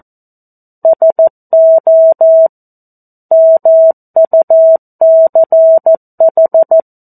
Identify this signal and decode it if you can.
Morse code, heard as a keyed tone, about 14 wpm, 650 Hz. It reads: SO MUCH
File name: 02-Chapter-02 14 WPM